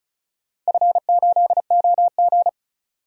Morse code, heard as a keyed tone, about 36 wpm, 700 Hz. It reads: F8OG
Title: DXCC-100 Call Signs - 4 Characters - 35wpm